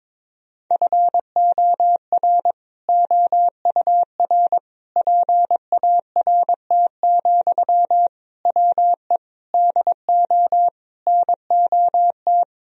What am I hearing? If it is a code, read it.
Morse code, heard as a keyed tone, about 22 wpm, 700 Hz. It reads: FOR OUR PART, WE DO NOT
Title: Sentences from Top 1000 Words - Review Part 1 - 22wpm